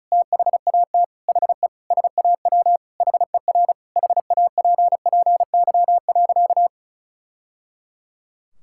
Morse code, heard as a keyed tone, about 34 wpm, 700 Hz. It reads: THAT HE SAW HER HAPPY.
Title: Sentences from Top 1000 Words - Review Part 6 - 35wpm